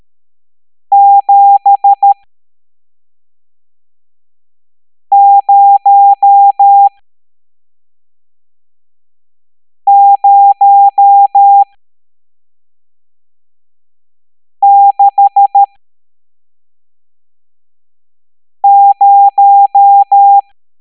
Morse code, heard as a keyed tone, about 13 wpm, spaced as for 3 wpm, 800 Hz. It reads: 70060
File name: hamdata / L11-EX01-67890